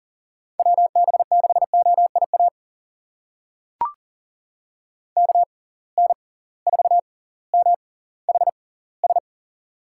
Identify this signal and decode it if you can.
Morse code, heard as a keyed tone, about 40 wpm, spaced as for 20 wpm, 700 Hz. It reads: WB6OIA KN4MHS
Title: USA Call Signs - 6 Characters - 6x Character Spacing - 40wpm